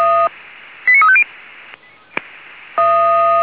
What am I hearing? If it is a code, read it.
Five-tone selective call (ZVEI), heard as quick runs of five tones, tones of about 70 ms. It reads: 89350